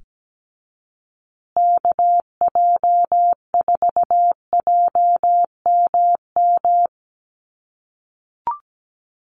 Morse code, heard as a keyed tone, about 17 wpm, 700 Hz. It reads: KJ4JMM
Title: USA Call Signs Repeated 3x 17wpm